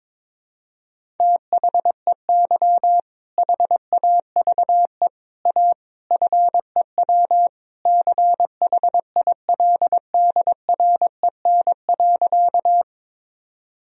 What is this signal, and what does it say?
Morse code, heard as a keyed tone, about 21 wpm, 700 Hz. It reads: THEY HAVE A FEW CHILDREN.
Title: Sentences from Top 1000 Words - Review Part 1 - 22wpm